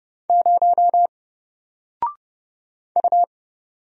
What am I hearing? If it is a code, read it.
Morse code, heard as a keyed tone, about 29 wpm, 700 Hz. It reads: 0 U
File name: Single-Letter-Number-30wpm